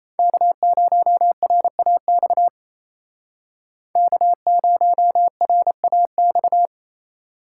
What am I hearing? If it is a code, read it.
Morse code, heard as a keyed tone, about 36 wpm, 700 Hz. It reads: K0RAX K0RAX